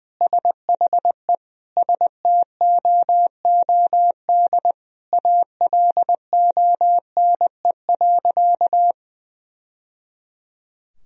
Morse code, heard as a keyed tone, about 20 wpm, 700 Hz. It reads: SHE STOOD ALONE.
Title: Sentences from Top 1000 Words - Review Part 2 - 20wpm